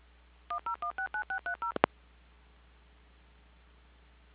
DTMF, heard as telephone touch tones, tones of about 110 ms. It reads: 10169630